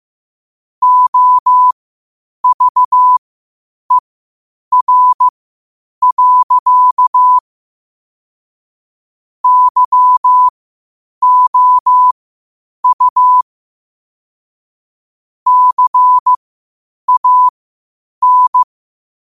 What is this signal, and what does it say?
Morse code, heard as a keyed tone, about 15 wpm, spaced as for 8 wpm, 1000 Hz. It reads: OVER. YOU CAN
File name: qotdfiles / 8wpm